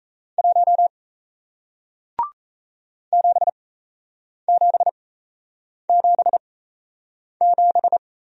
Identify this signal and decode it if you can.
Morse code, heard as a keyed tone, about 38 wpm, spaced as for 8 wpm, 700 Hz. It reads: 1 7777